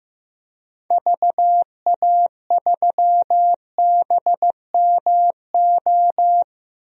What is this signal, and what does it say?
Morse code, heard as a keyed tone, about 15 wpm, 700 Hz. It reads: VA3BMO
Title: DXCC-100 Call Signs - 6 Characters - 15wpm